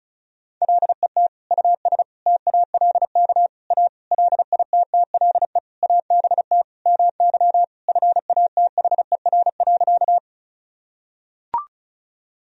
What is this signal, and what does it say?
Morse code, heard as a keyed tone, about 36 wpm, 700 Hz. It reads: LET US TALK A LITTLE ABT MY FATHER.
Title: Sentences from Top 400 Words and Abbreviations 35wpm